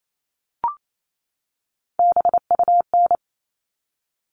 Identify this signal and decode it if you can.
Morse code, heard as a keyed tone, about 28 wpm, 700 Hz. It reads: BUN
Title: All Cut Numbers - 3-Digit 28wpm